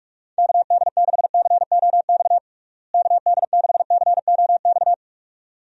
Morse code, heard as a keyed tone, about 41 wpm, 700 Hz. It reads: KD6COX KD6COX
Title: USA Call Signs - 6 Characters - Repeated 2x - 45wpm